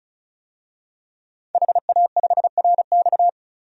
Morse code, heard as a keyed tone, about 34 wpm, 700 Hz. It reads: HA5RX
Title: DXCC-100 Call Signs - 5 Characters - 35wpm